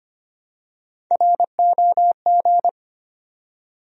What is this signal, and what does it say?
Morse code, heard as a keyed tone, about 25 wpm, 700 Hz. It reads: ROG
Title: USA Common Names - Max 5 Letters - 25wpm